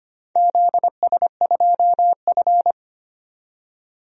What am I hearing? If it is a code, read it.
Morse code, heard as a keyed tone, about 25 wpm, 700 Hz. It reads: ZS2F
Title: DXCC-100 Call Signs - 4 Characters - 25wpm